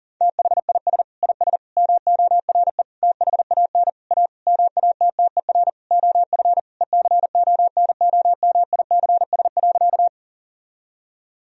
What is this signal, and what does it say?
Morse code, heard as a keyed tone, about 37 wpm, 700 Hz. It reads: THIS IS MORE THAN A MATTER OF ECONOMICS.
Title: Sentences from Top 1000 Words - Review Part 3 - 40wpm